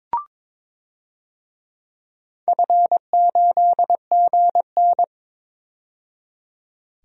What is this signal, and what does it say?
Morse code, heard as a keyed tone, about 22 wpm, 700 Hz. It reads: F8GN
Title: DXCC-100 Call Signs - 4 Characters - 22wpm